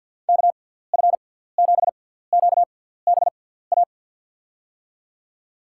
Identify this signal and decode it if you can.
Morse code, heard as a keyed tone, about 53 wpm, spaced as for 17 wpm, 700 Hz. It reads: KF7QBA
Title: USA Call Signs - 6 Characters - 6x Character Spacing - 50wpm